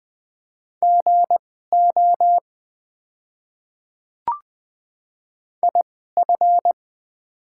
Morse code, heard as a keyed tone, about 20 wpm, spaced as for 14 wpm, 700 Hz. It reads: GO IF
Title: 2x Character Spacing - Top 100 Words - 20wpm